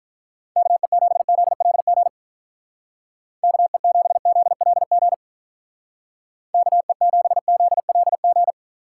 Morse code, heard as a keyed tone, about 43 wpm, 700 Hz. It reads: KE7ZLG KE7ZLG KE7ZLG
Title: Speed Racing - USA Call Signs 35wpm